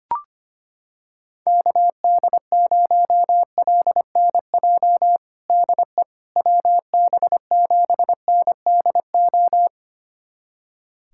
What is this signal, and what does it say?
Morse code, heard as a keyed tone, about 25 wpm, 700 Hz. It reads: KD0LNJ DE WB7NDO
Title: USA Callsign DE Callsign 25wpm